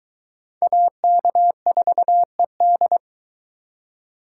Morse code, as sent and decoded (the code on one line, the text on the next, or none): .- -.- ....- . -..
AK4ED